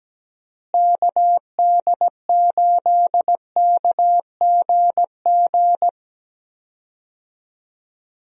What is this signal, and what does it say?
Morse code, heard as a keyed tone, about 17 wpm, 700 Hz. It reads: KD8KGG
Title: USA Call Signs - 6 Characters - Repeated 3x - 17wpm